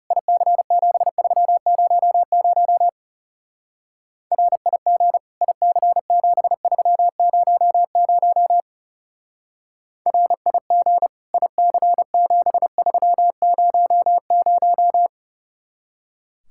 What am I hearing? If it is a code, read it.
Morse code, heard as a keyed tone, about 34 wpm, 700 Hz. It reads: IC7300 RIG IC7300 RIG IC7300
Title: Speed-Racing Common QSO Phrases Encore - 30wpm